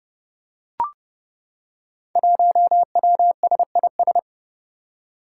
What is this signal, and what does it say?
Morse code, heard as a keyed tone, about 29 wpm, 700 Hz. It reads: 1WSIS
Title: Five Challenging Letter-Number Combinations 30wpm